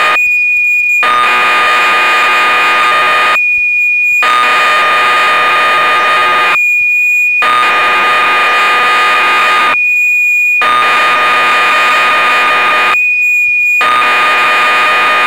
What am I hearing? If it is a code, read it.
Cassette data (Kansas City type).
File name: magazines / Acorn User / 1985-05 BBC Programs